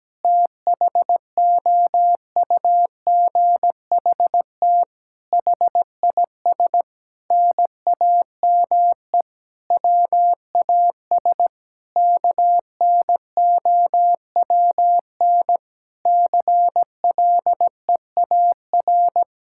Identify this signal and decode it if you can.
Morse code, heard as a keyed tone, about 17 wpm, 700 Hz. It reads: THOUGHT HIS NAME WAS KNOWN CLEAR
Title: Sentences from Top 1000 Words - Review Part 5 - 17wpm